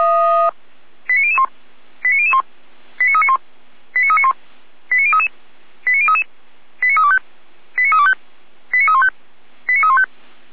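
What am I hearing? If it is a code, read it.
Five-tone selective call (ZVEI), heard as quick runs of five tones, tones of about 70 ms. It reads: 89001 89001 89391 89391 89033 89033 89325 89325 89315 89315
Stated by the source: ZVEI five-tone selective call 89050 received on SDR.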